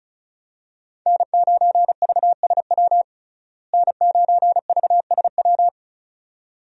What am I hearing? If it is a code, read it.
Morse code, heard as a keyed tone, about 34 wpm, 700 Hz. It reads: N9VSW N9VSW
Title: USA Call Signs - 5 Characters - Repeated 2x - 35wpm